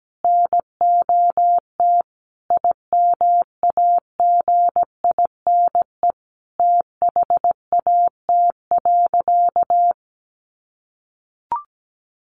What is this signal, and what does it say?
Morse code, heard as a keyed tone, about 17 wpm, 700 Hz. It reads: NOT IMAGINE THAT.
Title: Sentences from Top 1400 Words 17wpm